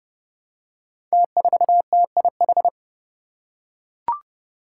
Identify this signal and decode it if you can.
Morse code, heard as a keyed tone, about 30 wpm, 700 Hz. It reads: T4TIH
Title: Five Challenging Letter-Number Combinations 30wpm